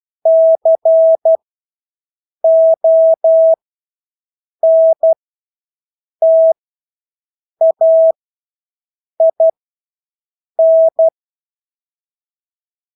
Morse code, heard as a keyed tone, about 12 wpm, spaced as for 6 wpm, 650 Hz. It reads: CONTAIN
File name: Engelske ord 12-6 wpm